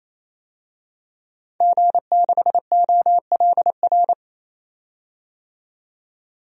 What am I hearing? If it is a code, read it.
Morse code, heard as a keyed tone, about 28 wpm, 700 Hz. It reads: G6OLR
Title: DXCC-100 Call Signs - 5 Characters - 28wpm